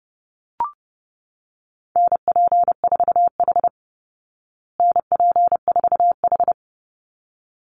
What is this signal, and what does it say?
Morse code, heard as a keyed tone, about 30 wpm, 700 Hz. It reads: NP4H NP4H
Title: Call Signs Repeated 3x - 30wpm